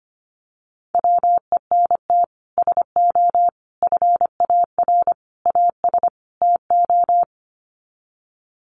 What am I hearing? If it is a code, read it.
Morse code, heard as a keyed tone, about 25 wpm, 700 Hz. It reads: WENT SO FAR AS TO
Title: Sets of 5 Words Encore 25wpm